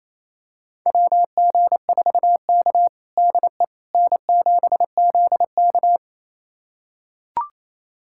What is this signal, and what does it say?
Morse code, heard as a keyed tone, about 28 wpm, 700 Hz. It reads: WG4K DE N7ZK
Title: USA Callsign DE Callsign - 4 Characters - 28wpm